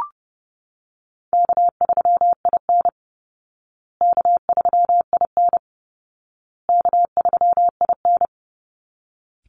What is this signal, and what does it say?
Morse code, heard as a keyed tone, about 30 wpm, 700 Hz. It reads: K3IN K3IN K3IN
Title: Easy Call Signs Repeated 3x - 30wpm